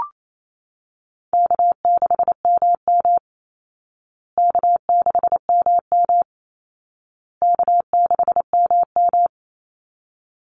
Morse code, heard as a keyed tone, about 28 wpm, 700 Hz. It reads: K6MM K6MM K6MM
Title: Easy Call Signs Repeated 3x - 28wpm